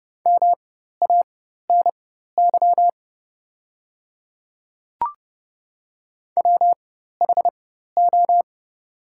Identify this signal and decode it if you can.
Morse code, heard as a keyed tone, about 30 wpm, spaced as for 14 wpm, 700 Hz. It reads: MANY WHO
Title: 4x Character Spacing - Top 200 Words - 30wpm